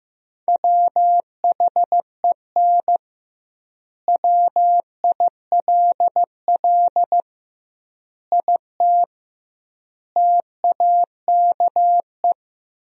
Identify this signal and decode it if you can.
Morse code, heard as a keyed tone, about 16 wpm, 700 Hz. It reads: WHEN WILL IT TAKE
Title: 2x Word-Spacing - Sentences from Top 200 Words - 15wpm